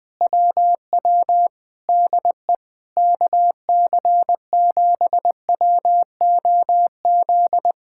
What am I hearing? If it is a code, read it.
Morse code, heard as a keyed tone, about 20 wpm, 700 Hz. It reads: WW DE KC7WOZ
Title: Speed-Racing Common QSO Phrases - 20wpm